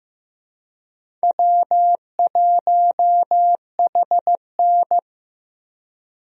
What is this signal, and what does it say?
Morse code, heard as a keyed tone, about 15 wpm, 700 Hz. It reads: W1HN
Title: USA Call Signs - 4 Characters - Repeated 3x - 15wpm